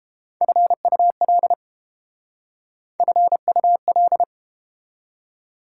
Morse code, heard as a keyed tone, about 29 wpm, 700 Hz. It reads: FUL FUL